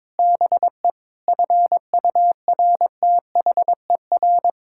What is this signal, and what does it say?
Morse code, heard as a keyed tone, about 22 wpm, 700 Hz. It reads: BE FURTHER
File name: practice / 22wpm / Sets of 8 Words - 22wpm